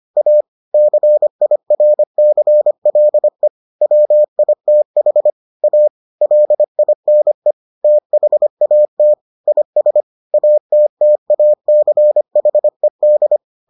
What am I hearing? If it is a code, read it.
Morse code, heard as a keyed tone, about 25 wpm, 600 Hz. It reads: A CIRCLE WITH A LINE THAT IS ATTACHED